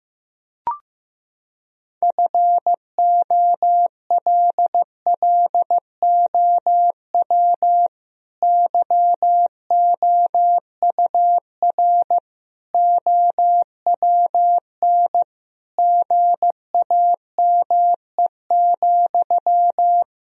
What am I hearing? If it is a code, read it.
Morse code, heard as a keyed tone, about 15 wpm, 700 Hz. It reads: FOLLOW YOUR OWN GAME,
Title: Sentences from Top 1000 Words - Review Part 3 - 15wpm